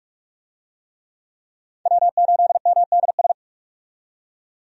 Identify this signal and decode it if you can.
Morse code, heard as a keyed tone, about 46 wpm, 700 Hz. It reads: W8MDS
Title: USA Call Signs - 5 Characters - 45wpm